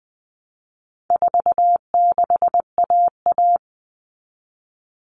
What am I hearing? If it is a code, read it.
Morse code, heard as a keyed tone, about 20 wpm, 700 Hz. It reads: 46AA
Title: All Cut Numbers - 4-Digit 20wpm